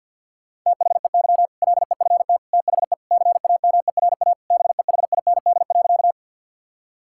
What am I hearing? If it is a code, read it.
Morse code, heard as a keyed tone, about 47 wpm, 700 Hz. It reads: THEY LEFT THE CAMERA BEHIND.
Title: Sentences from Top 1000 Words - Review Part 5 - 50wpm